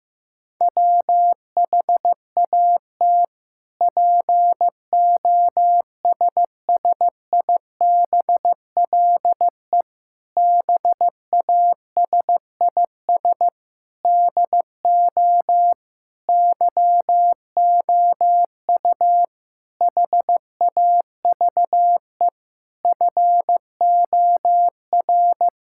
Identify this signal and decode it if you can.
Morse code, heard as a keyed tone, about 15 wpm, 700 Hz. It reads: WHAT POSSIBLE BASIS DO YOU HAVE FOR
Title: Sentences from Top 1000 Words - Review Part 5 - 15wpm